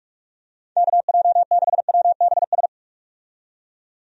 Morse code, heard as a keyed tone, about 45 wpm, 700 Hz. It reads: KJ6WBS
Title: USA Call Signs - 6 Characters - 45wpm